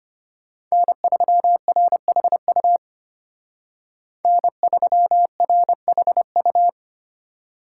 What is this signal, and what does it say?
Morse code, heard as a keyed tone, about 32 wpm, 700 Hz. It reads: N3RHU N3RHU